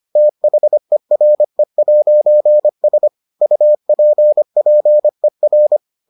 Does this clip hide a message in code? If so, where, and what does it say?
Yes: Morse code from 0.2 to 5.8 s, heard as a keyed tone, about 25 wpm, 600 Hz. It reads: THERE'S UPPER